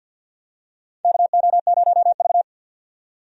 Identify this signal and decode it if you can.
Morse code, heard as a keyed tone, about 51 wpm, 700 Hz. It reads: KO0V
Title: USA Call Signs - 4 Characters - 50wpm